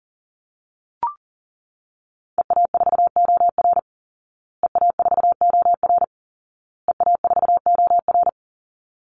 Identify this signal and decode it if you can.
Morse code, heard as a keyed tone, about 40 wpm, 700 Hz. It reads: EA4OR EA4OR EA4OR
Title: Call Signs Repeated 2x - 40wpm